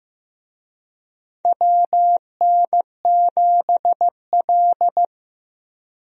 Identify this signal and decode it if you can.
Morse code, heard as a keyed tone, about 15 wpm, 700 Hz. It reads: WN7L
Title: USA Call Signs - 4 Characters - 15wpm